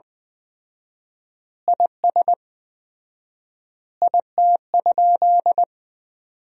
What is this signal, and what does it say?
Morse code, heard as a keyed tone, about 20 wpm, 700 Hz. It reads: IS IT?
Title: 4x Word-Spacing - Sentences from Top 100 Words - 20wpm